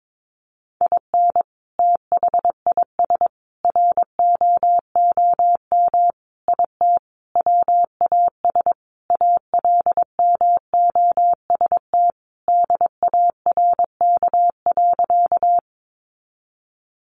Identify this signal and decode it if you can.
Morse code, heard as a keyed tone, about 22 wpm, 700 Hz. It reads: IN THIS ROOM IT WAS ALMOST DARK.
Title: Sentences from Top 700 Words 22wpm